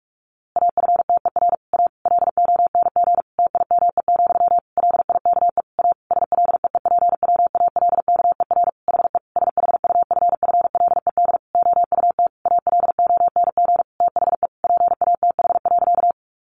Morse code, heard as a keyed tone, about 44 wpm, 700 Hz. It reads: AFTER A LONG TIME, LIKE A SLEEPWALKER HE SHUFFLED OUT ALONG THE PATH.
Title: A Sound Of Thunder 45wpm